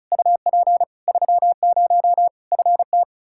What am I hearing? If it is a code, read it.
Morse code, heard as a keyed tone, about 36 wpm, 700 Hz. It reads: UP 30 FT
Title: Practice QSO - Exchange 2 - Info - 35wpm